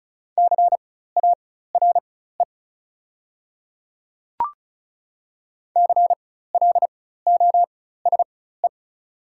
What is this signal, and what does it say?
Morse code, heard as a keyed tone, about 36 wpm, spaced as for 16 wpm, 690 Hz. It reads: CARE CLOSE
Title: 4x Character Spacing - Top 300 Words - 35wpm